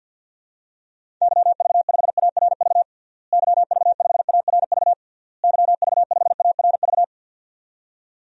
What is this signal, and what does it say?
Morse code, heard as a keyed tone, about 51 wpm, 700 Hz. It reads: YV5ARV YV5ARV YV5ARV
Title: DXCC-100 Call Signs - 6 Characters - Repeated 3x - 50wpm